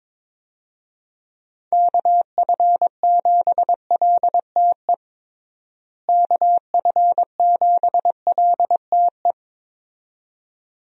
Morse code, heard as a keyed tone, about 21 wpm, 700 Hz. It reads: KF7LTE KF7LTE
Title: USA Call Signs - 6 Characters - Repeated 2x - 22wpm